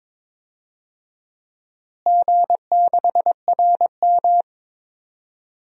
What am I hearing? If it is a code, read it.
Morse code, heard as a keyed tone, about 21 wpm, 700 Hz. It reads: G6RM
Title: DXCC-100 Call Signs - 4 Characters - Repeated 3x - 22wpm